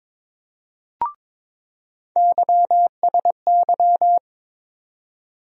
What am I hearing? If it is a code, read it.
Morse code, heard as a keyed tone, about 21 wpm, 700 Hz. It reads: YSY
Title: USA Call Sign Postfixes - 3 Letters 22wpm